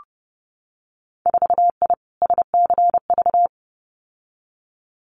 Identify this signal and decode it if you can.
Morse code, heard as a keyed tone, about 30 wpm, 700 Hz. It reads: 4I SCV
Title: Winter Field Day Exchanges Encore 30wpm